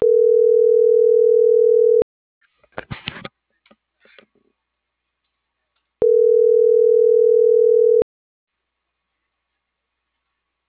Ringback tone (North America), 440 + 480 Hz.